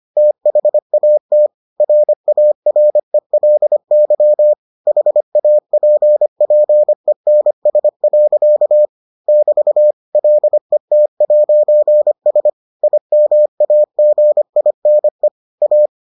Morse code, heard as a keyed tone, about 25 wpm, 600 Hz. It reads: THAT RARELY HAPPENS. <BT> LET'S IMAGINE A